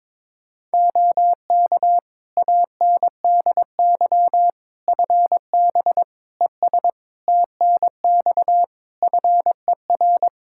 Morse code, heard as a keyed tone, about 22 wpm, 700 Hz. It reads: OK ANDY FB ES TNX FER
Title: Practice QSO - Exchange 2 - Info - 22wpm